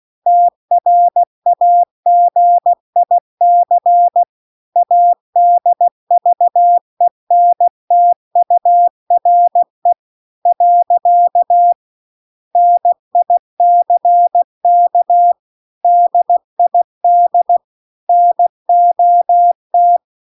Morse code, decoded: TRAGIC ADVENTURE . NICK DID NOT

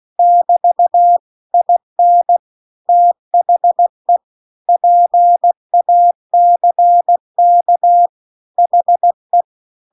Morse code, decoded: <BT> IN THE PACK HE

